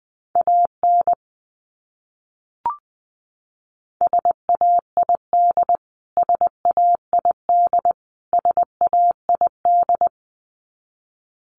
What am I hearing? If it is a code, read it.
Morse code, heard as a keyed tone, about 20 wpm, 700 Hz. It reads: AN SAID SAID SAID
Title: Top 500 Words Repeated 20wpm